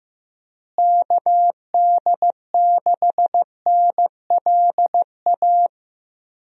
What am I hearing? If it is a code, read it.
Morse code, heard as a keyed tone, about 15 wpm, 700 Hz. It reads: KD6NLA
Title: USA Call Signs - 6 Characters - Repeated 2x - 15wpm